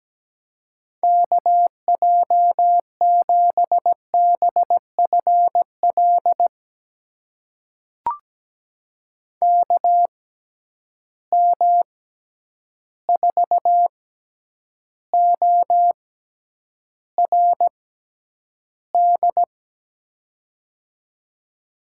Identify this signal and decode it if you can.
Morse code, heard as a keyed tone, about 17 wpm, spaced as for 9 wpm, 700 Hz. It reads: KJ7BFL KM4ORD
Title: USA Call Signs - 6 Characters - 6x Character Spacing - 17wpm